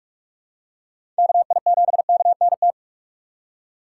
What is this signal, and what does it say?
Morse code, heard as a keyed tone, about 42 wpm, 700 Hz. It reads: KI7KNT
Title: USA Call Signs - 6 Characters - 4x Character Spacing - 45wpm